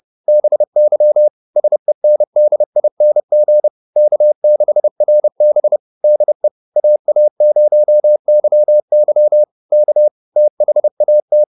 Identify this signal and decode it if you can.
Morse code, heard as a keyed tone, about 30 wpm, 600 Hz. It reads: BY SENDING K6RB DE AA0YY K THAT